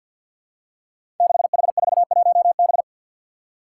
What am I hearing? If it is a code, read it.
Morse code, heard as a keyed tone, about 47 wpm, 700 Hz. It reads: 6H41B
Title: Five Challenging Letter-Number Combinations 50wpm